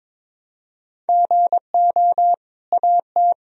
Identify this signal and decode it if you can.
Morse code, heard as a keyed tone, about 22 wpm, 700 Hz. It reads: GO AT
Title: Sentences from Top 1000 Words - Review Part 1 - 22wpm